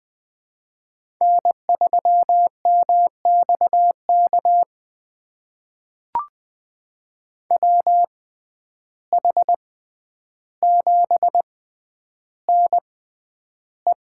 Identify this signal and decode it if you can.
Morse code, heard as a keyed tone, about 20 wpm, spaced as for 10 wpm, 700 Hz. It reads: N3MXK WH7NE
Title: USA Call Signs - 5 Characters - 6x Character Spacing - 20wpm